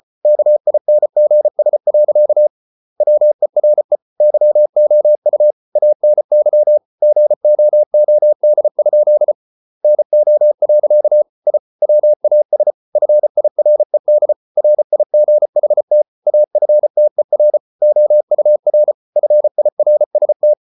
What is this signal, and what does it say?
Morse code, heard as a keyed tone, about 34 wpm, 600 Hz. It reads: KINGS. WERE YOU ANY GOOD? NO. I WAS FIRED RIGHT AFTER OUR FIRST